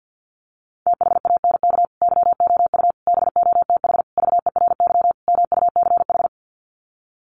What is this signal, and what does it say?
Morse code, heard as a keyed tone, about 50 wpm, 700 Hz. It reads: THANK YOU BOTH VERY MUCH
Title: Sets of 5 Words 50wpm